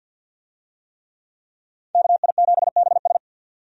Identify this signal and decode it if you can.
Morse code, heard as a keyed tone, about 53 wpm, 700 Hz. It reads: KI7BS
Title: USA Call Signs - 5 Characters - 50wpm